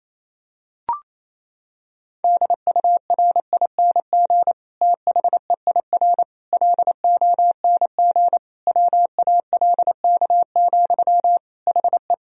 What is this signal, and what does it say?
Morse code, heard as a keyed tone, about 28 wpm, 700 Hz. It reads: DURING THEIR LONG WALK, HE
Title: Sentences from Top 1000 Words - Review Part 2 - 28wpm